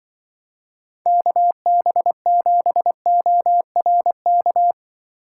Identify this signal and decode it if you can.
Morse code, heard as a keyed tone, about 24 wpm, 700 Hz. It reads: KB7ORK
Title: Speed Racing - USA 2x3 Call Signs 20wpm